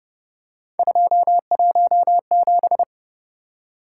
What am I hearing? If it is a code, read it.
Morse code, heard as a keyed tone, about 29 wpm, 700 Hz. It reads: 217